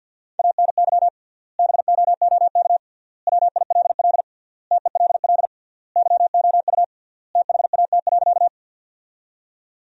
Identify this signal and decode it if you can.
Morse code, heard as a keyed tone, about 50 wpm, 700 Hz. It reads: ANY BOOK WILL TELL YOU THAT.
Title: 3x Word-Spacing - Sentences from Top 300 Words - 50wpm